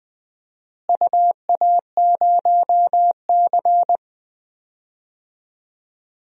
Morse code, heard as a keyed tone, about 20 wpm, 700 Hz. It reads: UA0C
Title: DXCC-100 Call Signs - 4 Characters - Repeated 2x - 20wpm